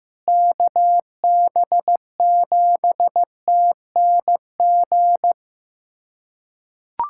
Morse code, heard as a keyed tone, about 15 wpm, 700 Hz. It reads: KB7TNG
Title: USA Callsign DE Callsign - 6 Characters - 15wpm